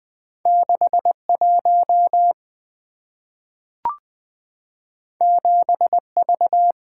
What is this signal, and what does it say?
Morse code, heard as a keyed tone, about 20 wpm, 690 Hz. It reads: 61 7V
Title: Two Letter-Number Combinations - 20wpm